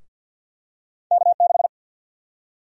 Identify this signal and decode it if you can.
Morse code, heard as a keyed tone, about 47 wpm, 700 Hz. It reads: K6
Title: USA Call Sign Prefixes - 50wpm